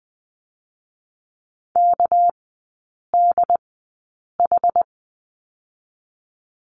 Morse code, heard as a keyed tone, about 20 wpm, 700 Hz. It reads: K D H
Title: 198 Lesson 32e - Letter k - Three Characters 20wpm